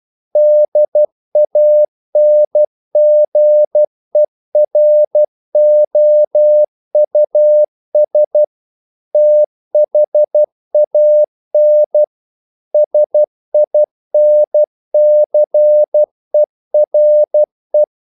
Morse code, heard as a keyed tone, about 12 wpm, 600 Hz. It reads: DANGEROUS THAN SINCERE
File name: CW / Practice / EnglishQuotations0112